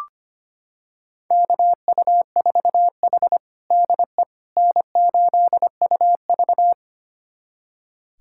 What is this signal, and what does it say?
Morse code, heard as a keyed tone, about 25 wpm, 700 Hz. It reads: KU4H DE N8UV
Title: USA Callsign DE Callsign - 4 Characters - 25wpm